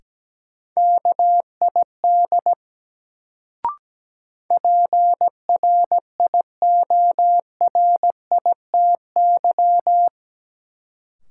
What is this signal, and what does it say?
Morse code, heard as a keyed tone, about 17 wpm, 700 Hz. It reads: KID PRIORITY
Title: Sentences from Top 1600 Words - New Word Drill - 17wpm